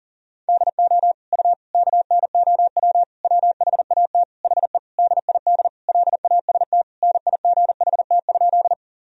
Morse code, heard as a keyed tone, about 40 wpm, 700 Hz. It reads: DO U KNOW WHAT HE DID LAST NIGHT?
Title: Sentences from Top 300 Words and Abbreviations 40wpm